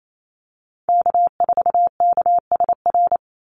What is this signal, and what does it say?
Morse code, heard as a keyed tone, about 28 wpm, 700 Hz. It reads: K4KSR
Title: Call Signs Repeated 3x - 28wpm